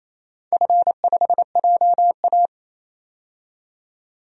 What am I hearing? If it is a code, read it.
Morse code, heard as a keyed tone, about 27 wpm, 700 Hz. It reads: F5JA
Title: DXCC-100 Call Signs - 4 Characters - 28wpm